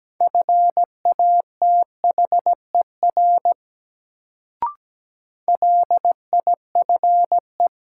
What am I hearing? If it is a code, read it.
Morse code, heard as a keyed tone, about 17 wpm, 700 Hz. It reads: FATHER LIFE
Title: Sentences from Top 300 Words - New Word Drill - 17wpm